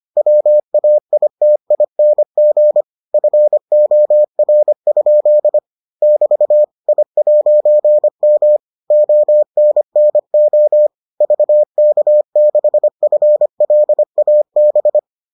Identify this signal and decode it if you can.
Morse code, heard as a keyed tone, about 25 wpm, 600 Hz. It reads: WAITING FOR? <BT> I'M ONNO VK6FLAB